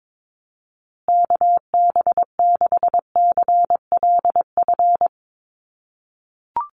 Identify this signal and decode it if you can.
Morse code, heard as a keyed tone, about 22 wpm, 700 Hz. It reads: KB6CLF
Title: USA Call Signs Repeated 3x 22wpm